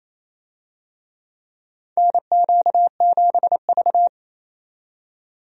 Morse code, heard as a keyed tone, about 28 wpm, 700 Hz. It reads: NQ7V
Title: USA Call Signs - 4 Characters - 28wpm